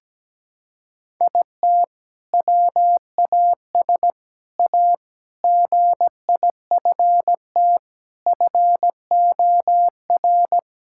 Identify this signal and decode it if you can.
Morse code, heard as a keyed tone, about 17 wpm, 700 Hz. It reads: IT WAS A GIFT FOR
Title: Sentences from Top 1000 Words - Review Part 6 - 17wpm